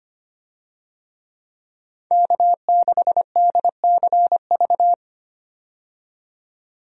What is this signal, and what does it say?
Morse code, heard as a keyed tone, about 25 wpm, 700 Hz. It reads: K6DCV
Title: USA Call Signs - 5 Characters - 25wpm